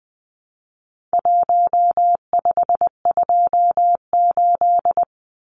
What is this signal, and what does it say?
Morse code, heard as a keyed tone, about 20 wpm, 700 Hz. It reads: 1528
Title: Numbers - Contest Style - 4-Digit 20wpm